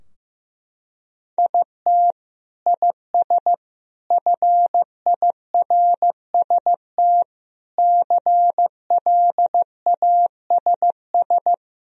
Morse code, heard as a keyed tone, about 15 wpm, 700 Hz. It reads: IT IS FIRST CLASS
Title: Sentences from Top 1000 Words - Review Part 3 - 15wpm